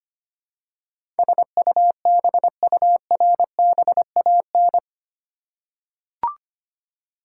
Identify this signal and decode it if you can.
Morse code, heard as a keyed tone, about 25 wpm, 700 Hz. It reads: SUBURBAN
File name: Eight Letter Words - 25wpm